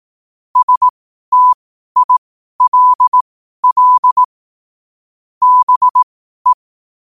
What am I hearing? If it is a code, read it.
Morse code, heard as a keyed tone, about 18 wpm, spaced as for 12 wpm, 1000 Hz. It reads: STILL BE
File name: qotdfiles / 12wpm